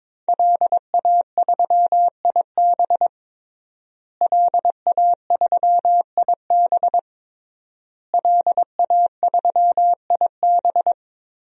Morse code, heard as a keyed tone, about 22 wpm, 700 Hz. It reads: LA3IB LA3IB LA3IB
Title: DXCC-100 Call Signs - 5 Characters - Repeated 3x - 22wpm